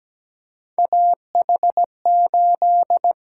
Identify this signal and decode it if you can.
Morse code, heard as a keyed tone, about 17 wpm, 700 Hz. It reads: AH8
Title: USA Call Signs - 8x Character Spacing - 17wpm